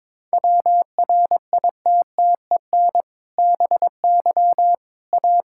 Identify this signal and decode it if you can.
Morse code, heard as a keyed tone, about 21 wpm, 700 Hz. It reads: WRITTEN BY A
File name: practice / 22wpm / Sentences with 2-Word Phrases - 22wpm